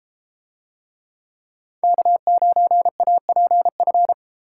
Morse code, heard as a keyed tone, about 31 wpm, 700 Hz. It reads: K9APF